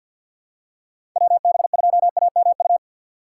Morse code, heard as a keyed tone, about 47 wpm, 690 Hz. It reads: WB2AMU
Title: USA Call Signs - 6 Characters - 3x Character Spacing - 50wpm